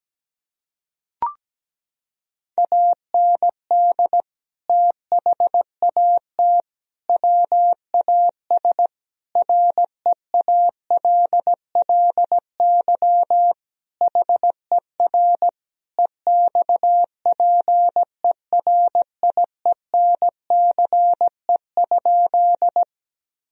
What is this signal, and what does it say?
Morse code, heard as a keyed tone, about 17 wpm, 700 Hz. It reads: AND THAT WAS REALLY HER EXPERIENCE?
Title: Sentences from Top 1000 Words - Review Part 3 - 17wpm